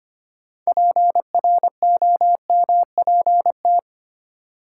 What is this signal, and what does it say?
Morse code, heard as a keyed tone, about 25 wpm, 700 Hz. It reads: PROMPT
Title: Six Letter Words - 25wpm